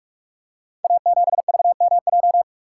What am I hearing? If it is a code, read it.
Morse code, heard as a keyed tone, about 46 wpm, 700 Hz. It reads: A7VMJ
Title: Code Groups - 5 Letter:Numbers - 45wpm